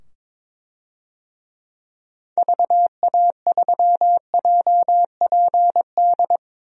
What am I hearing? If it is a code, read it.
Morse code, heard as a keyed tone, about 22 wpm, 700 Hz. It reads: VA3JPD
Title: DXCC-100 Call Signs - 6 Characters - 22wpm